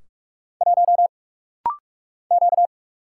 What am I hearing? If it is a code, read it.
Morse code, heard as a keyed tone, about 44 wpm, 700 Hz. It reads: 1 Q
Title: Single Letter-Number - Rapid-Fire 45wpm